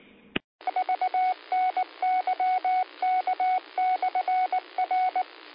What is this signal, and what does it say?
Morse code, heard as a keyed tone, about 19 wpm, 740 Hz. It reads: VNYK/R
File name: qsorec SimplexLogic